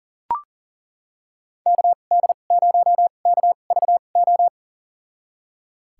Morse code, heard as a keyed tone, about 40 wpm, spaced as for 29 wpm, 700 Hz. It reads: KD0KVO
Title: USA Call Signs - 6 Characters - 2x Character Spacing - 40wpm